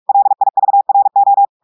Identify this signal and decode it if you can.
Morse code, heard as a keyed tone, about 46 wpm, 800 Hz. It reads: LIVRO
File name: portugais texte en clair 1 vitesse 45 mots minutes